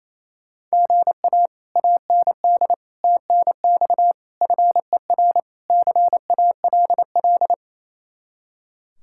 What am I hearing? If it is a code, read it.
Morse code, heard as a keyed tone, about 36 wpm, 700 Hz. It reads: GA AND TNX FER CALL